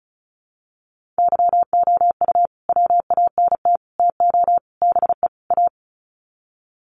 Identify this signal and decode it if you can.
Morse code, heard as a keyed tone, about 34 wpm, 700 Hz. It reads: YOU WANT TO BE A
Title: Sets of 5 Words Encore2 35wpm